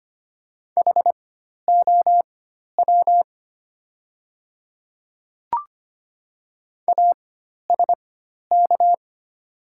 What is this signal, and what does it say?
Morse code, heard as a keyed tone, about 25 wpm, spaced as for 12 wpm, 700 Hz. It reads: HOW ASK
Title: 4x Character Spacing - Top 300 Words - 25wpm